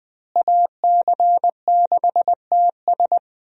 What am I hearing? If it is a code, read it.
Morse code, heard as a keyed tone, about 20 wpm, 700 Hz. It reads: AC6TS
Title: USA Call Signs - 5 Characters - Repeated 2x - 20wpm